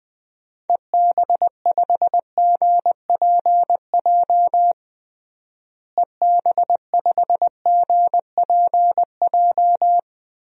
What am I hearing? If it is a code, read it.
Morse code, heard as a keyed tone, about 20 wpm, 700 Hz. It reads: EB5GPJ EB5GPJ
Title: DXCC-100 Call Signs - 6 Characters - Repeated 2x - 20wpm